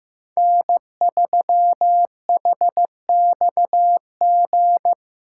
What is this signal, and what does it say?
Morse code, heard as a keyed tone, about 15 wpm, 700 Hz. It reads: N3HXG
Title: Practice QSO - Exchange 4 - Ending - 15wpm